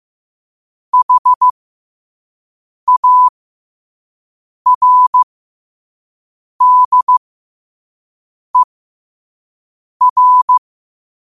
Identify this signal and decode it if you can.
Morse code, heard as a keyed tone, about 15 wpm, spaced as for 5 wpm, 1000 Hz. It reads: HARDER
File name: qotdfiles / 5wpm